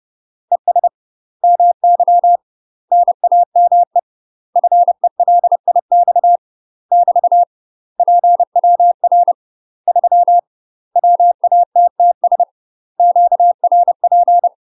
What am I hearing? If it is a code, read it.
Morse code, heard as a keyed tone, about 30 wpm, 700 Hz. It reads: ES MY NAME FELIX <BT> PWR 3 WATTS QRP
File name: UK MASTER QSOs TRAINING / 30WPM / UK MASTER QSOs 0026